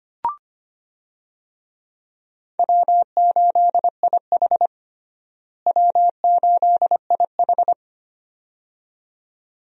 Morse code, heard as a keyed tone, about 25 wpm, 700 Hz. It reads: W8IH W8IH
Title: USA Call Signs - 4 Characters - Repeated 2x - 25wpm